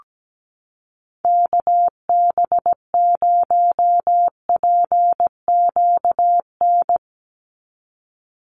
Morse code, heard as a keyed tone, about 17 wpm, 700 Hz. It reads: KB0PQN
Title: USA Call Signs 17wpm